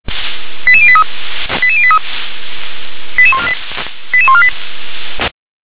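Five-tone selective call (ZVEI), heard as quick runs of five tones, tones of about 70 ms. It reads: 88083 88083 88148
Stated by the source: ZVEI five-tone selective call received via SDR.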